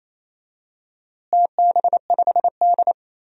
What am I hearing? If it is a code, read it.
Morse code, heard as a keyed tone, about 28 wpm, 700 Hz. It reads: TB5D
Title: DXCC-100 Call Signs - 4 Characters - Repeated 3x - 28wpm